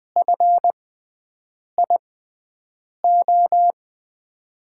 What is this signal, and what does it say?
Morse code, heard as a keyed tone, about 20 wpm, spaced as for 7 wpm, 700 Hz. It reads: FIO
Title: USA Call Signs - 6 Characters - 6x Character Spacing - 20wpm